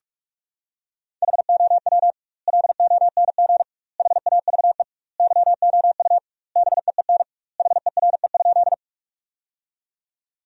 Morse code, heard as a keyed tone, about 46 wpm, 700 Hz. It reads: HOW LONG HAVE YOU BEEN HERE?
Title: 2x Word-Spacing - Sentences from Top 200 Words - 45wpm